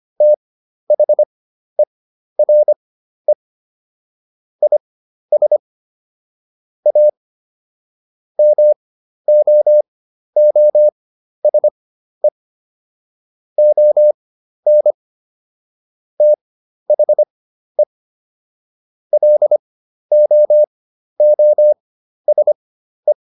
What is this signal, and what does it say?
Morse code, heard as a keyed tone, about 25 wpm, spaced as for 12 wpm, 600 Hz. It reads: THERE IS A MOOSE ON THE LOOSE